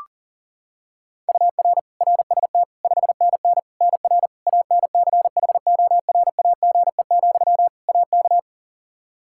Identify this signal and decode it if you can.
Morse code, heard as a keyed tone, about 40 wpm, 700 Hz. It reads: UR RST 5NN NR ANCHORAGE, AK
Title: Practice QSO - Exchange 1 - Beginning - 40wpm